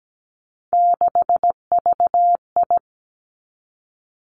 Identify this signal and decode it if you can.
Morse code, heard as a keyed tone, about 17 wpm, 700 Hz. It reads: 6VI
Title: Three Challenging Letter-Number Combinations 17wpm